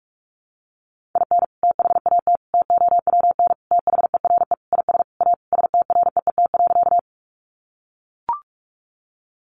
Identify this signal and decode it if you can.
Morse code, heard as a keyed tone, about 44 wpm, 700 Hz. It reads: IN THAT TOWN THERE IS A STREET.
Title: Sentences from Top 500 Words 45wpm